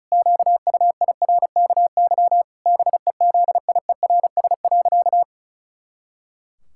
Morse code, heard as a keyed tone, about 36 wpm, 700 Hz. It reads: QUIRKY BEZIERS.
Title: Pangram Challenge v2 - 35wpm